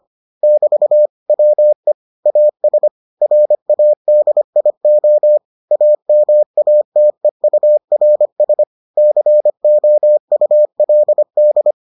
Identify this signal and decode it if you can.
Morse code, heard as a keyed tone, about 25 wpm, 600 Hz. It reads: <BT> WE AS RADIO AMATEURS COULD